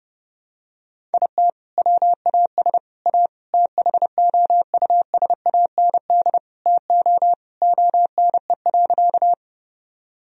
Morse code, heard as a keyed tone, about 32 wpm, 700 Hz. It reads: IT WAS A THOUSAND TO ONE.
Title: Sentences from Top 1000 Words - Review Part 3 - 30wpm